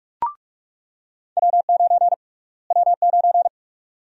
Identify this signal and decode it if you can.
Morse code, heard as a keyed tone, about 46 wpm, 700 Hz. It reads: W9 W9
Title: USA Call Sign Prefixes - Repeated 2x - 45wpm